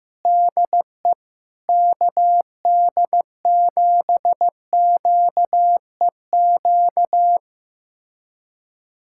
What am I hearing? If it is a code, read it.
Morse code, heard as a keyed tone, about 15 wpm, 700 Hz. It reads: DE KD7QEQ
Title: USA Callsign DE Callsign 15wpm